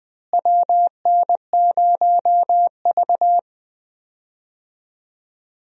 Morse code, heard as a keyed tone, about 20 wpm, 700 Hz. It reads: WN0V
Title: USA Call Signs - 4 Characters - Repeated 2x - 20wpm